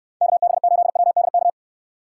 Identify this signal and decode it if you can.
Morse code, heard as a keyed tone, about 69 wpm, 700 Hz. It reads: KB8WDK